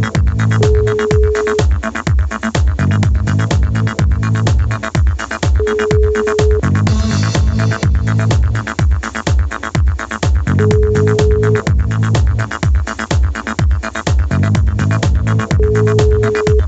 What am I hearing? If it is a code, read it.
Ringback tone (Europe, 425 Hz), with noise.